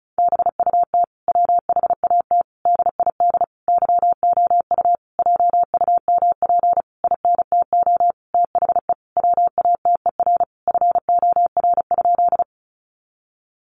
Morse code, heard as a keyed tone, about 34 wpm, 700 Hz. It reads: BUT WHAT DID YOU JUMP INTO THE WATER FOR?
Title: Sentences from Top 1300 Words 35wpm